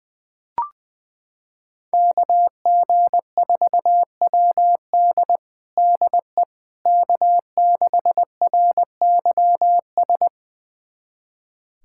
Morse code, heard as a keyed tone, about 20 wpm, 700 Hz. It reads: KG4WD DE K6RYS
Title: USA Callsign DE Callsign - 5 Characters - 20wpm